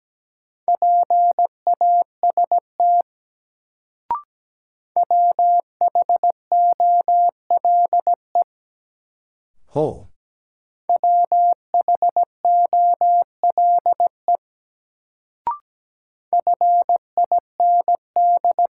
Morse code, heard as a keyed tone, about 17 wpm, 700 Hz. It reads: PAST WHOLE WHOLE FIND
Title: Sets of 2 Words - New Word Drill - 17wpm